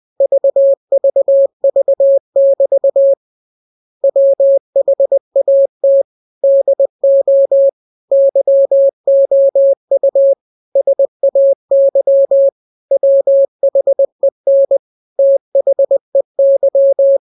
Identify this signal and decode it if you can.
Morse code, heard as a keyed tone, about 20 wpm, 550 Hz. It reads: VVV<BT> WHAT DO YOU SAY WHEN THEY